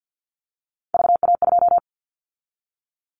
Morse code, heard as a keyed tone, about 50 wpm, 700 Hz. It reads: VA2
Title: Callsign Prefixes for Top 100 DXCC - 50wpm